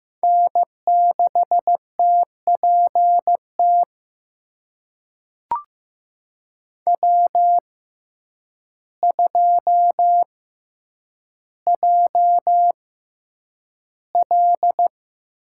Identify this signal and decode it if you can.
Morse code, heard as a keyed tone, about 15 wpm, spaced as for 8 wpm, 700 Hz. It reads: N6TPT W2JL
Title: USA Call Signs - 5 Characters - 6x Character Spacing - 15wpm